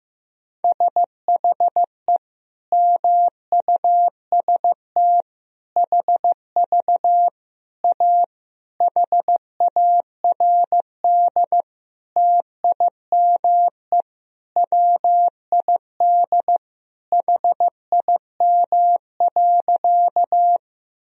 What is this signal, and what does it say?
Morse code, heard as a keyed tone, about 15 wpm, 700 Hz. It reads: SHE MUST HV A HARD TIME WID HIM.
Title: Sentences from Top 300 Words and Abbreviations 15wpm